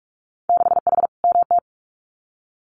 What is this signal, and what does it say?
Morse code, heard as a keyed tone, about 44 wpm, 690 Hz. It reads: 6H MT
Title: Winter Field Day Exchanges Encore 45wpm